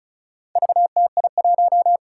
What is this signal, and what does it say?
Morse code, heard as a keyed tone, about 36 wpm, 700 Hz. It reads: VTI1
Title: Four Challenging Letter-Number Combinations 35wpm